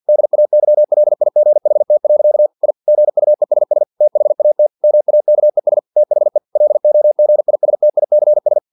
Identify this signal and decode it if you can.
Morse code, heard as a keyed tone, about 50 wpm, 600 Hz. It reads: DAYLIGHT. I GUESS THAT MAKES THE LOGISTICS
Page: Never 49wpm